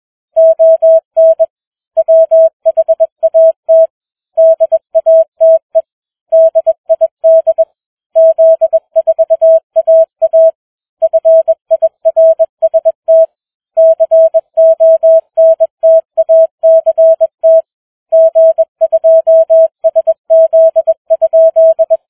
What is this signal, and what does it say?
Morse code, heard as a keyed tone, about 21 wpm, 650 Hz. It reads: ON WHAT DATE DID Z4AA FIRST CONTACT G2SZ?